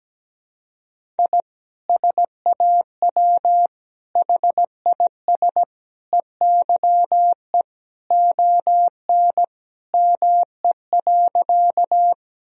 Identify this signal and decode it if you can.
Morse code, heard as a keyed tone, about 17 wpm, 700 Hz. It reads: I SAW HIS EYE ON ME.
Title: Sentences from Top 1000 Words - Review Part 1 - 17wpm